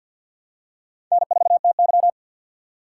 Morse code, heard as a keyed tone, about 49 wpm, 700 Hz. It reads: N4TY
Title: USA Call Signs - 4 Characters - 2x Character Spacing - 50wpm